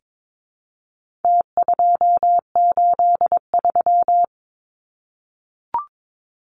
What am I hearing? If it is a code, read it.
Morse code, heard as a keyed tone, about 22 wpm, 700 Hz. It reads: T283
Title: Numbers - Contest Style - 4-Digit 22wpm